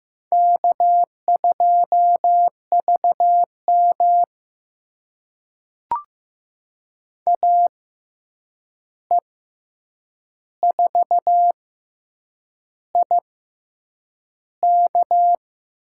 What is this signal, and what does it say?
Morse code, heard as a keyed tone, about 15 wpm, spaced as for 7 wpm, 700 Hz. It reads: K2VM AE4IK
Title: USA Call Signs - 6x Character Spacing - 15wpm